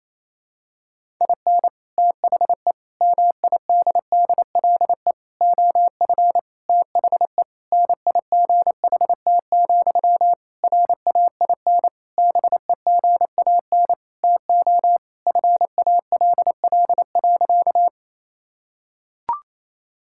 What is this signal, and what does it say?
Morse code, heard as a keyed tone, about 28 wpm, 700 Hz. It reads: IN THE MIDDLE OF THE NIGHT, RAIN BEGAN TO FALL.
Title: Sentences from Top 1000 Words - Review Part 6 - 28wpm